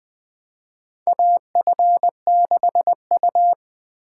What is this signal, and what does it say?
Morse code, heard as a keyed tone, about 21 wpm, 700 Hz. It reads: AF6U
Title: USA Call Signs - 6x Character Spacing - 20wpm